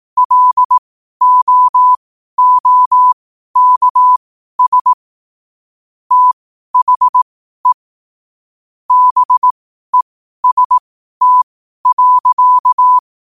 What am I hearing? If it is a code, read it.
Morse code, heard as a keyed tone, about 18 wpm, spaced as for 12 wpm, 1000 Hz. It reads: LOOKS THE BEST.
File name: qotdfiles / 12wpm